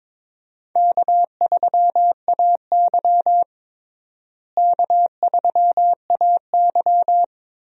Morse code, heard as a keyed tone, about 22 wpm, 700 Hz. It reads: K3AY K3AY
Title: USA Call Signs - 4 Characters - Repeated 2x - 22wpm